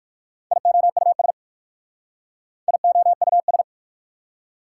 Morse code, heard as a keyed tone, about 60 wpm, 690 Hz. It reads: IOUS IOUS